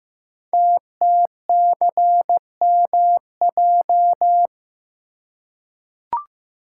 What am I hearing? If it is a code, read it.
Morse code, heard as a keyed tone, about 15 wpm, 700 Hz. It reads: TTCMJ
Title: Code Groups - 5 Letters - 15wpm